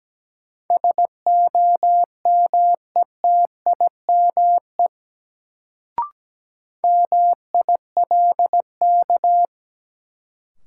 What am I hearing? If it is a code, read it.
Morse code, heard as a keyed tone, about 17 wpm, 700 Hz. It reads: SOMETIME MILK
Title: Sentences from Top 1500 Words - New Word Drill Alt - 17wpm